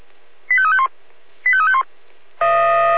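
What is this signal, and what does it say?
Five-tone selective call (ZVEI), heard as quick runs of five tones, tones of about 70 ms. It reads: 85351 85351
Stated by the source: ZVEI five-tone selective call received on SDR, code 25414.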